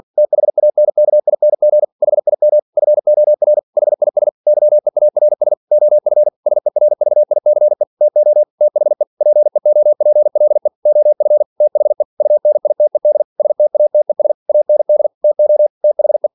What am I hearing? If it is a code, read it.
Morse code, heard as a keyed tone, about 46 wpm, 600 Hz. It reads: THANKING HIM FOR HIS YEARS OF SERVICE TO THE PEOPLE OF THE UNITED STATES AND TO THE
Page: Never 47wpm